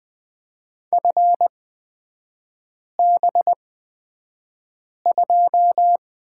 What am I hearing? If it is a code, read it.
Morse code, heard as a keyed tone, about 20 wpm, spaced as for 5 wpm, 700 Hz. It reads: FB2
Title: 224 Lesson 36e - BT - Three Characters 20wpm